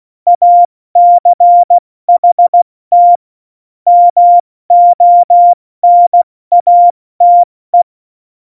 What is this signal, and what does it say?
Morse code, heard as a keyed tone, about 16 wpm, 700 Hz. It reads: ACHT MONATE
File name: Morsekurs / Witze / Witz 01. Dezember 2020, 70 F80